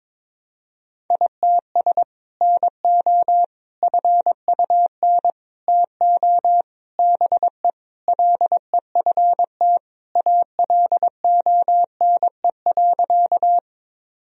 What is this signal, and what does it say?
Morse code, heard as a keyed tone, about 22 wpm, 700 Hz. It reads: ITS NO FUN TO BE LEFT ALONE.
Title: Sentences from Top 1000 Words - Review Part 4 - 22wpm